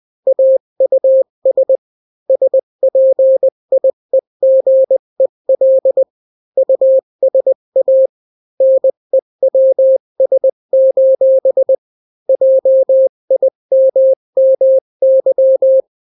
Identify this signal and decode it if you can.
Morse code, heard as a keyed tone, about 20 wpm, 530 Hz. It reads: AUS SPIEGEL USA NEWS: JIMMY
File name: daily / 18wpm